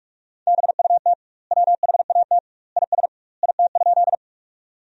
Morse code, heard as a keyed tone, about 46 wpm, 700 Hz. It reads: BUT WHAT IS IT?
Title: 2x Word-Spacing - Sentences from Top 100 Words - 45wpm